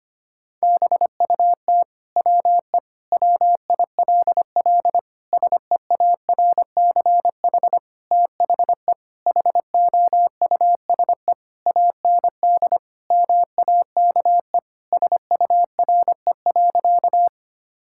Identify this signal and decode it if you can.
Morse code, heard as a keyed tone, about 25 wpm, 700 Hz. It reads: BUT WE WILL SEARCH THE HOUSE AND MAKE SURE.
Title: Sentences from Top 1000 Words - Review Part 5 - 25wpm